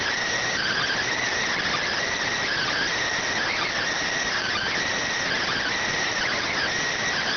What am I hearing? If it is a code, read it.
Emergency Alert System (SAME) tones (header bursts), heard as harsh buzzing bursts. It is heard from 0.5 to 6.6 s.